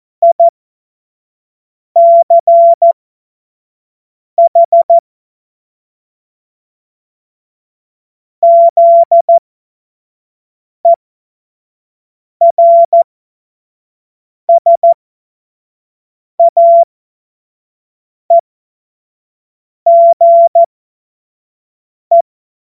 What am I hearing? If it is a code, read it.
Morse code, heard as a keyed tone, about 14 wpm, spaced as for 5 wpm, 680 Hz. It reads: ICH ZERSAEGE
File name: Morsekurs / Witze / Witz 13. Januar 2026 25 F70